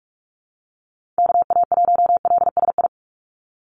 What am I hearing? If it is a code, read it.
Morse code, heard as a keyed tone, about 46 wpm, 690 Hz. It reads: KA1LSI
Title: USA Call Signs 45wpm